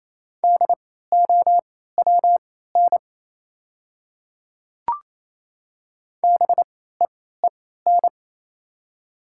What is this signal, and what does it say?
Morse code, heard as a keyed tone, about 28 wpm, spaced as for 16 wpm, 700 Hz. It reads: DOWN BEEN
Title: 3x Character Spacing - Top 100 Words - 28wpm